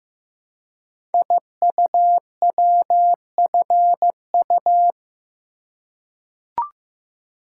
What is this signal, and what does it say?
Morse code, heard as a keyed tone, about 15 wpm, 700 Hz. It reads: IUWFU
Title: Code Groups - 5 Letters - 15wpm